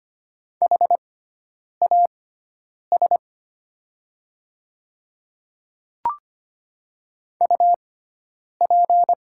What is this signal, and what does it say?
Morse code, heard as a keyed tone, about 25 wpm, spaced as for 9 wpm, 700 Hz. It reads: HAS UP
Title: 6x Character Spacing - Top 300 Words - 25wpm